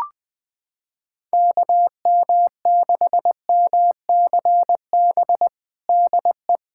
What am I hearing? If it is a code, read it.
Morse code, heard as a keyed tone, about 21 wpm, 700 Hz. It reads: KM6MCB DE
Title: USA Callsign DE Callsign 20wpm